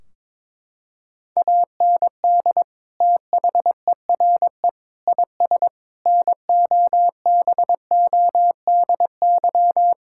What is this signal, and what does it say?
Morse code, heard as a keyed tone, about 22 wpm, 700 Hz. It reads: AND THERE IS NOBODY
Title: Sentences from Top 1600 Words 22wpm